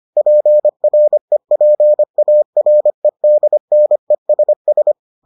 Morse code, heard as a keyed tone, about 25 wpm, 600 Hz. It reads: PREPAREDNESS